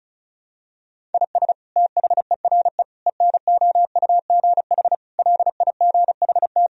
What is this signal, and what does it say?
Morse code, heard as a keyed tone, about 34 wpm, 700 Hz. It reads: IS THERE ENOUGH LIGHT
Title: Sentences from Top 1000 Words - Review Part 1 - 35wpm